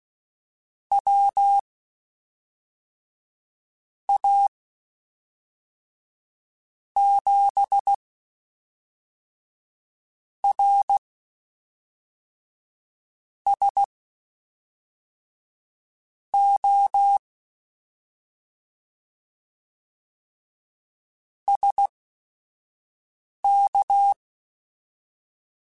Morse code, heard as a keyed tone, about 16 wpm, spaced as for 3 wpm, 800 Hz. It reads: WA7RSO SK
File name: (spaced) / QSOs Spaced-mp3s / kc7sac @5 sp